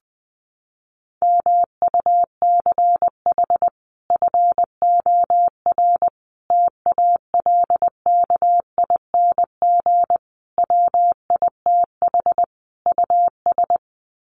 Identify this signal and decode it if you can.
Morse code, heard as a keyed tone, about 19 wpm, 700 Hz. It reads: MUCH FOR TALKING WITH US
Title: Sets of 5 Words Encore 20wpm